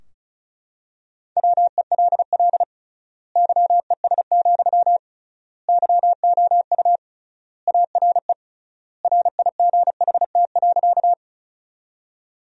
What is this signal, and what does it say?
Morse code, heard as a keyed tone, about 36 wpm, 700 Hz. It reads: WELL YES, YOU ARE RIGHT.
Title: 3x Word-Spacing - Sentences from Top 500 Words - 35wpm